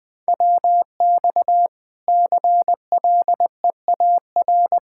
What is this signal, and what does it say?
Morse code, heard as a keyed tone, about 21 wpm, 700 Hz. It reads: WX CLEAR
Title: Practice QSO - Exchange 2 - Info - 20wpm